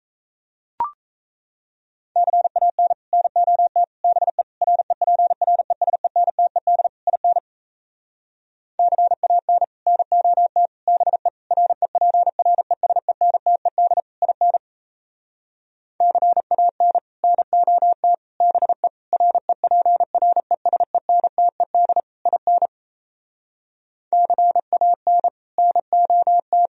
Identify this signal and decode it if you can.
Morse code, heard as a keyed tone, about 37 wpm, 700 Hz. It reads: CAN NOT BE REPRESENTED IN CAN NOT BE REPRESENTED IN CAN NOT BE REPRESENTED IN CAN NOT